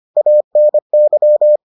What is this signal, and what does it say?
Morse code, heard as a keyed tone, about 25 wpm, 600 Hz. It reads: ANY